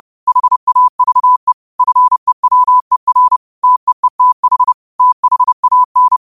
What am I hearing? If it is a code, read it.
Morse code, heard as a keyed tone, about 30 wpm, 1000 Hz. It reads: HAVE FEWER TEETH THAN